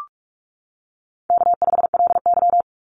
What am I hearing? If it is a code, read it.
Morse code, heard as a keyed tone, about 44 wpm, 700 Hz. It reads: K5LY
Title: Call Signs - 45wpm